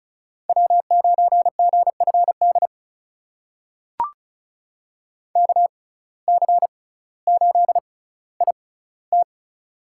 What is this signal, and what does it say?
Morse code, heard as a keyed tone, about 36 wpm, spaced as for 18 wpm, 700 Hz. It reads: W9GFD KC8IT
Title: USA Call Signs - 5 Characters - 6x Character Spacing - 35wpm